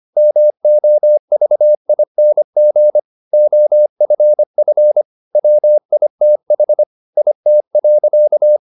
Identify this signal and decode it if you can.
Morse code, heard as a keyed tone, about 25 wpm, 600 Hz. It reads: MOVING OFF WITH IT.
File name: the old man and the sea part 1 25wpm